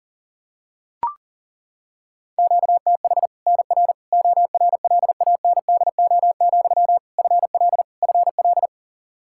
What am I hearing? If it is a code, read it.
Morse code, heard as a keyed tone, about 40 wpm, 700 Hz. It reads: QTH NR ORLANDO, FL FL
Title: Practice QSO - Exchange 1 - Beginning - 40wpm